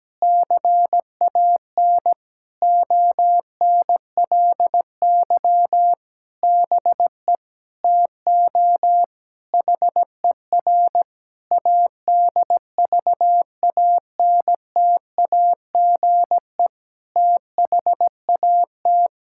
Morse code, heard as a keyed tone, about 17 wpm, 700 Hz. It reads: CAN ONLY BE TO HER ADVANTAGE THAT
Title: Sentences from Top 1000 Words - Review Part 6 - 17wpm